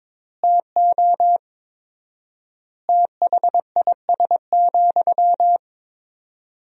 Morse code, heard as a keyed tone, about 22 wpm, 700 Hz. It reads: TO THIS,
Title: 4x Word-Spacing - Sentences from Top 400 Words - 22wpm